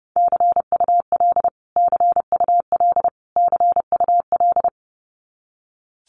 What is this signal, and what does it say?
Morse code, heard as a keyed tone, about 30 wpm, 700 Hz. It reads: CUL CUL CUL
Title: Abbreviations Repeated 3x - 30wpm